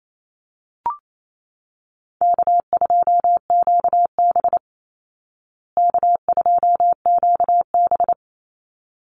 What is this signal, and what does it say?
Morse code, heard as a keyed tone, about 28 wpm, 700 Hz. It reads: K2QB K2QB
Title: Easy Call Signs Repeated 2x - 28wpm